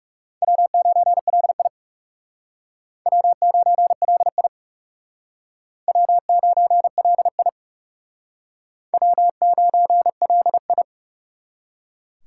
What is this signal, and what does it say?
Morse code, heard as a keyed tone, about 40 wpm, 700 Hz. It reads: W9LI W9LI W9LI W9LI